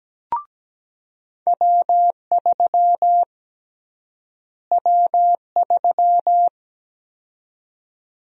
Morse code, heard as a keyed tone, about 17 wpm, 700 Hz. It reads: W3 W3
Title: USA Call Sign Prefixes - Repeated 2x - 17wpm